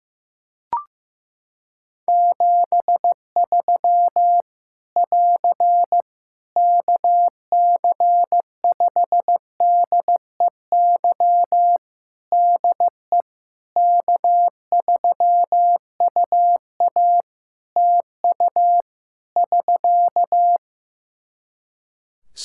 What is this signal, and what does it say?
Morse code, heard as a keyed tone, about 15 wpm, 700 Hz. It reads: 73 <AR> KC5DEY DE K3UA TU <SK>
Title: Practice QSO - Exchange 4 - Ending - 15wpm